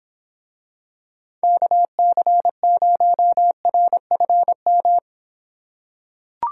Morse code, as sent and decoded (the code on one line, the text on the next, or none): -.- -.-. ----- .-. ..-. --
KC0RFM